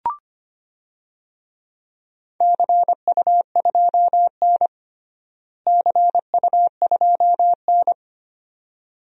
Morse code, heard as a keyed tone, about 25 wpm, 700 Hz. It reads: CU2N CU2N
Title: DXCC-100 Call Signs - 4 Characters - Repeated 2x - 25wpm